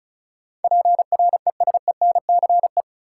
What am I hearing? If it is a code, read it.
Morse code, heard as a keyed tone, about 35 wpm, 700 Hz. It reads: PRESENCE
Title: Sets of 3 Words - New Word Drill - 35wpm